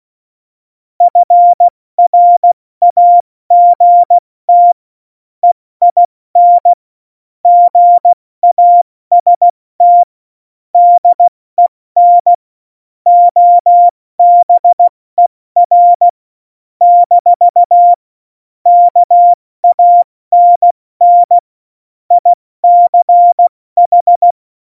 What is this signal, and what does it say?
Morse code, heard as a keyed tone, about 16 wpm, spaced as for 14 wpm, 700 Hz. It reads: FRAGT EIN GAST DEN OBER - KANN ICH
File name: Morsekurs / Witze / Witz 09. April 2019, 70 F80